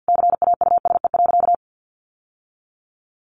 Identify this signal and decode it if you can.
Morse code, heard as a keyed tone, about 48 wpm, 700 Hz. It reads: CAUSE.
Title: Sentences from Top 700 Words 50wpm